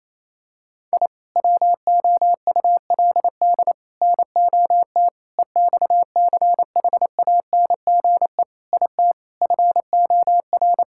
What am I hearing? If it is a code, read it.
Morse code, heard as a keyed tone, about 28 wpm, 700 Hz. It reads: I WOULD NOT EXCHANGE IT FOR
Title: Sentences from Top 1000 Words - Review Part 6 - 28wpm